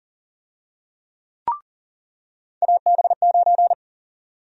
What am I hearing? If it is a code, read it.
Morse code, heard as a keyed tone, about 40 wpm, 700 Hz. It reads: AB9
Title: USA Call Sign Prefixes - 40wpm